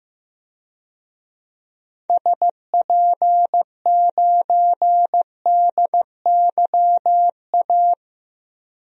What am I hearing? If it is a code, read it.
Morse code, heard as a keyed tone, about 15 wpm, 700 Hz. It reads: SP9DYA
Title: DXCC-100 Call Signs - 6 Characters - 15wpm